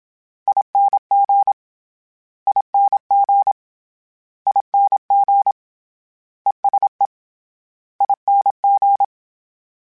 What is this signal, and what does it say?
Morse code, heard as a keyed tone, about 26 wpm, 800 Hz. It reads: ING ING ING ESE ING
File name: Short Runs / Short Runs-mp3s / ESE ING @25